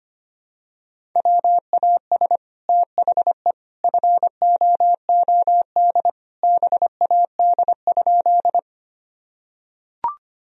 Morse code, heard as a keyed tone, about 25 wpm, 700 Hz. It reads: WAS THE FOOD BAD?
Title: Sentences from Top 1000 Words - Review Part 1 - 25wpm